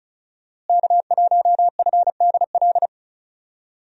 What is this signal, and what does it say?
Morse code, heard as a keyed tone, about 36 wpm, 700 Hz. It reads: K1FDL
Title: USA Call Signs - 5 Characters - Repeated 2x - 35wpm